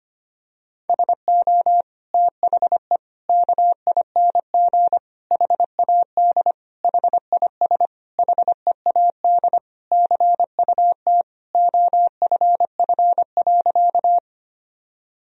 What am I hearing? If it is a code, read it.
Morse code, heard as a keyed tone, about 25 wpm, 700 Hz. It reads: SO THE KING HAD HIS HEAD CUT OFF.
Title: Sentences from Top 1000 Words - Review Part 1 - 25wpm